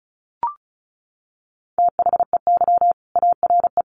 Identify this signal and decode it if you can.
Morse code, heard as a keyed tone, about 34 wpm, 700 Hz. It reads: THEY ARE
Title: Sentences from Top 200 Words 35wpm